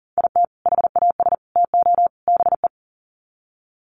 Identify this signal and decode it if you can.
Morse code, heard as a keyed tone, about 40 wpm, 700 Hz. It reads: IT HAS TO BE
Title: Sets of 4 Words 40wpm